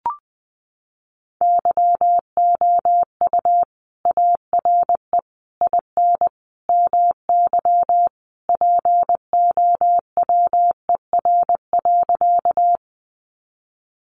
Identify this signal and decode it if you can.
Morse code, heard as a keyed tone, about 20 wpm, 700 Hz. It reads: YOU ARE IN MY POWER.
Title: Sentences from Top 500 Words 20wpm